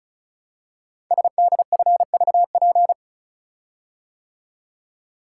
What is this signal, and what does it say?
Morse code, heard as a keyed tone, about 36 wpm, 700 Hz. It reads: SDFVP
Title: Code Groups - 5 Letter:Numbers - 35wpm